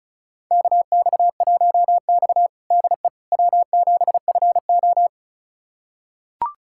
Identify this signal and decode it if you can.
Morse code, heard as a keyed tone, about 36 wpm, 700 Hz. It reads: KX1X DE W7FO
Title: USA Callsign DE Callsign - 4 Characters - 35wpm